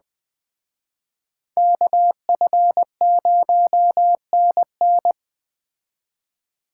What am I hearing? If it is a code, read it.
Morse code, heard as a keyed tone, about 20 wpm, 700 Hz. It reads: KF0NN
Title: Speed Racing - USA 2x2 Call Signs 17wpm